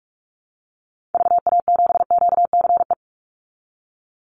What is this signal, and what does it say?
Morse code, heard as a keyed tone, about 44 wpm, 700 Hz. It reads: VA7QCE
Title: Call Signs Repeated 3x - 45wpm